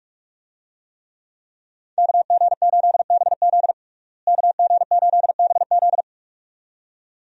Morse code, heard as a keyed tone, about 46 wpm, 700 Hz. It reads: KG8BZ KG8BZ
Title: USA Call Signs - 5 Characters - Repeated 2x - 45wpm